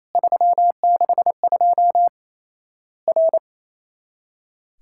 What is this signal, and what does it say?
Morse code, heard as a keyed tone, about 28 wpm, 700 Hz. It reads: 362 R
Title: Worked All Europe DX Contest - QTC Exchange - 28wpm